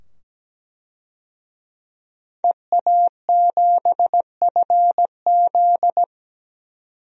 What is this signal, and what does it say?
Morse code, heard as a keyed tone, about 17 wpm, 700 Hz. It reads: EA7FZ
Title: DXCC-100 Call Signs - 5 Characters - 17wpm